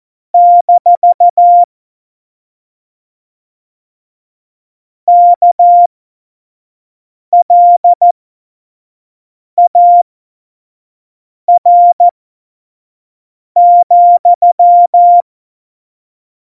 Morse code, heard as a keyed tone, about 14 wpm, spaced as for 5 wpm, 700 Hz. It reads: - KLAR,
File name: Morsekurs / Witze / Witz 20. November 2018, 25 F70